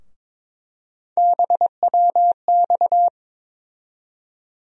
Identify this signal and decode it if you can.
Morse code, heard as a keyed tone, about 22 wpm, 700 Hz. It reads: BWX
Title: USA Call Sign Postfixes - 3 Letters 22wpm